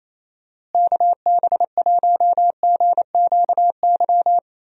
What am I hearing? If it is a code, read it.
Morse code, heard as a keyed tone, about 27 wpm, 700 Hz. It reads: KB1GQY
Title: USA Call Signs - 6 Characters - Repeated 3x - 28wpm